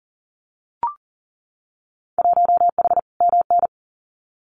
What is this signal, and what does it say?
Morse code, heard as a keyed tone, about 41 wpm, 700 Hz. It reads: 1H MN
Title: Winter Field Day Exchanges Encore 40wpm